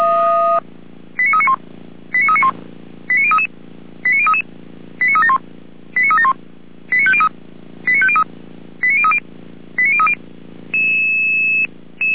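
Five-tone selective call (ZVEI), heard as quick runs of five tones, tones of about 70 ms. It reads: 89391 89391 89033 89033 89371 89371 89503 89503 89030 89030
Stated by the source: ZVEI five-tone selective call 89125 received on SDR.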